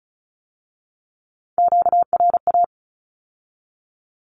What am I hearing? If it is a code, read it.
Morse code, heard as a keyed tone, about 36 wpm, 700 Hz. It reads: QRA
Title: Q-Codes with Spoken Meaning - 35wpm